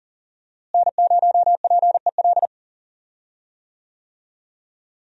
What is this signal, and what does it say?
Morse code, heard as a keyed tone, about 39 wpm, 700 Hz. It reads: N0PEL
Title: USA Call Signs - 5 Characters - Repeated 3x - 40wpm